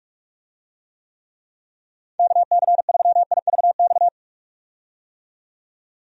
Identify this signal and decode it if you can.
Morse code, heard as a keyed tone, about 46 wpm, 700 Hz. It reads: KC3IVX
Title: USA Call Signs - 6 Characters - 45wpm